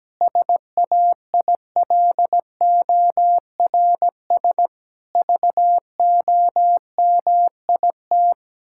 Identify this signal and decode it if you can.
Morse code, heard as a keyed tone, about 17 wpm, 700 Hz. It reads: SAILORS VOMIT
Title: Pangram Challenge v2 - 17wpm